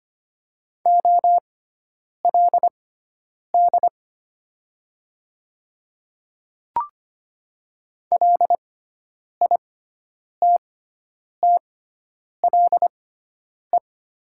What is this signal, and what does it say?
Morse code, heard as a keyed tone, about 25 wpm, spaced as for 9 wpm, 700 Hz. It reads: OLD LITTLE
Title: 6x Character Spacing - Top 200 Words - 25wpm